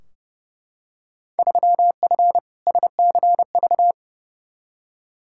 Morse code, heard as a keyed tone, about 31 wpm, 700 Hz. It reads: 3F SCV
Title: Field Day Exchanges - 30wpm